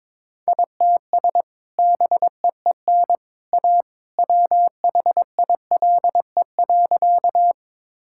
Morse code, heard as a keyed tone, about 22 wpm, 700 Hz. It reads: ITS BEEN A WHILE.
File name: practice / 22wpm / Sentences with 2-Word Phrases - 22wpm